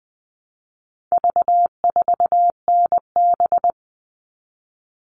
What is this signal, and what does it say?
Morse code, heard as a keyed tone, about 20 wpm, 700 Hz. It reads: V4NB
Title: All Cut Numbers - 4-Digit 20wpm